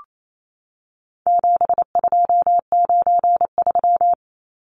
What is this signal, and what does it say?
Morse code, heard as a keyed tone, about 28 wpm, 700 Hz. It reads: 7293
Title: Numbers - 4-Digit 28wpm